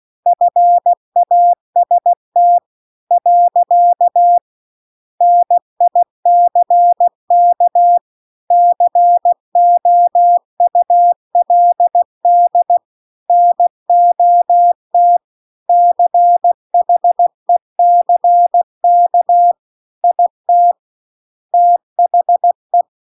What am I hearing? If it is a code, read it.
Morse code, heard as a keyed tone, about 16 wpm, 700 Hz. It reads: FAST . NICK COULD NOT CHECK IT THE
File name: CW-The-Big-Twoheaded-River-2-16WPM